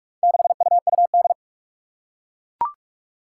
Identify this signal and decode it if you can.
Morse code, heard as a keyed tone, about 44 wpm, 700 Hz. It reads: 6UUD
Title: Four Challenging Letter-Number Combinations 45wpm